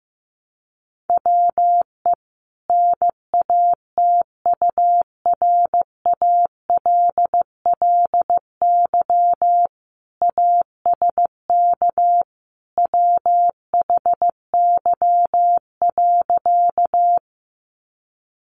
Morse code, heard as a keyed tone, about 15 wpm, 700 Hz. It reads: WE NATURALLY ASK WHY.
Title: Sentences from Top 1300 Words 15wpm